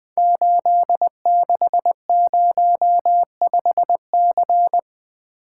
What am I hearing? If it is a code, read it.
Morse code, heard as a keyed tone, about 20 wpm, 700 Hz. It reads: 8605C
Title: Code Groups - 5 Letter:Numbers - 20wpm